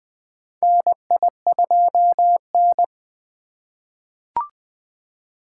Morse code, heard as a keyed tone, about 20 wpm, 700 Hz. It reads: NI2N
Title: USA Call Signs - 4 Characters - 4x Character Spacing - 20wpm